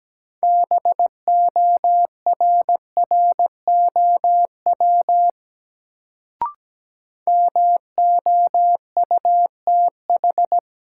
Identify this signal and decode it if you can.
Morse code, heard as a keyed tone, about 17 wpm, 700 Hz. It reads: BORROW MOUTH
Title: Sentences from Top 1500 Words - New Word Drill Alt - 17wpm